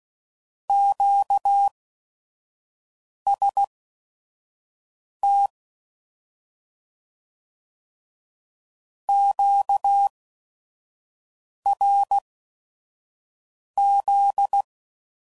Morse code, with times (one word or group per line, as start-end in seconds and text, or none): 0.7-5.5 QST
9.1-14.6 QRZ